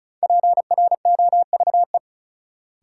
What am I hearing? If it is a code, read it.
Morse code, heard as a keyed tone, about 34 wpm, 700 Hz. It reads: PROVE
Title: Sentences from Top 1600 Words - New Word Drill - 35wpm